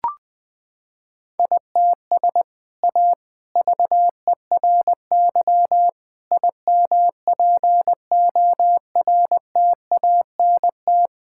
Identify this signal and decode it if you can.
Morse code, heard as a keyed tone, about 20 wpm, 700 Hz. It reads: ITS A VERY IMPORTANT
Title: Sentences from Top 1000 Words - Review Part 5 - 20wpm